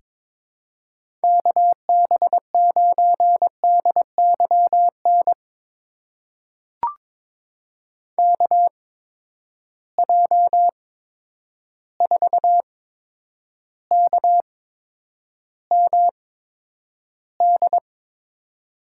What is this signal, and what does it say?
Morse code, heard as a keyed tone, about 22 wpm, spaced as for 9 wpm, 700 Hz. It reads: KB9DYN KJ4KMD
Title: USA Call Signs - 6 Characters - 8x Character Spacing - 22wpm